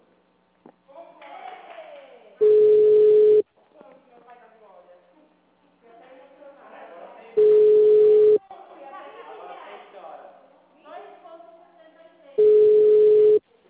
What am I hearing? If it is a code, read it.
Ringback tone (Europe, 425 Hz), with speech.